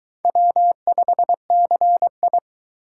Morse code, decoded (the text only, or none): W5CI